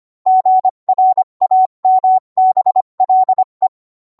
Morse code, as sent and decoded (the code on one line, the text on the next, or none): --. .-. .- -- -... .-.. .
GRAMBLE